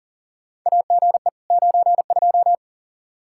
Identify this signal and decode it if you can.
Morse code, heard as a keyed tone, about 40 wpm, 700 Hz. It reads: AGE 92
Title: Practice QSO - Exchange 3 - More Info - 40wpm